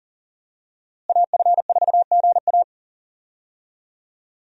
Morse code, heard as a keyed tone, about 38 wpm, 700 Hz. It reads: AF4GA
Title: USA Call Signs - 5 Characters - Repeated 2x - 40wpm